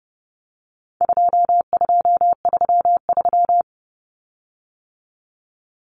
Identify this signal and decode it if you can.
Morse code, heard as a keyed tone, about 30 wpm, 700 Hz. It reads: 2233
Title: Common Cut Numbers - 4-Digit 30wpm